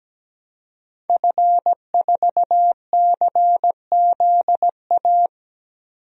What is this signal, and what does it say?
Morse code, heard as a keyed tone, about 17 wpm, 700 Hz. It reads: F4CZA
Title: DXCC-100 Call Signs - 5 Characters - 17wpm